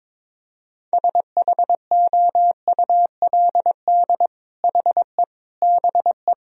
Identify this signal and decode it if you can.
Morse code, heard as a keyed tone, about 21 wpm, 700 Hz. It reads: SHOULD HE BE
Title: Sentences from Top 1000 Words - Review Part 4 - 22wpm